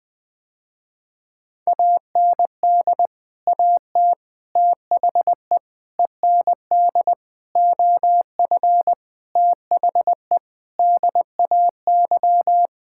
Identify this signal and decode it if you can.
Morse code, heard as a keyed tone, about 20 wpm, 700 Hz. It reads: AND AT THE END OF THE DAY
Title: Sentences with 5-Word Phrases - Spoken Beginning - 20wpm